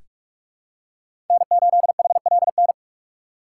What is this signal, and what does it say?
Morse code, heard as a keyed tone, about 46 wpm, 700 Hz. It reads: N8HLN
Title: USA Call Signs - 5 Characters - 4x Character Spacing - 45wpm